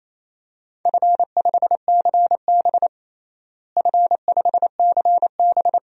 Morse code, heard as a keyed tone, about 27 wpm, 700 Hz. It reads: F5CB F5CB
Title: DXCC-100 Call Signs - 4 Characters - Repeated 2x - 28wpm